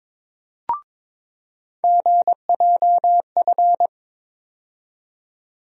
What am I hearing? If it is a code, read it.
Morse code, heard as a keyed tone, about 22 wpm, 700 Hz. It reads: GJF
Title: USA Call Sign Postfixes - 3 Letters 22wpm